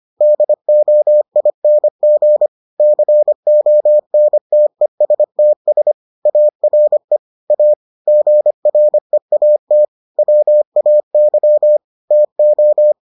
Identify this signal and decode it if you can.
Morse code, heard as a keyed tone, about 25 wpm, 600 Hz. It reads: DOING CONTESTS ARE A GREAT WAY TO